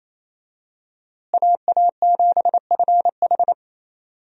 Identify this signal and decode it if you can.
Morse code, heard as a keyed tone, about 28 wpm, 700 Hz. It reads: AA7FH
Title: USA Call Signs - 5 Characters - Repeated 3x - 28wpm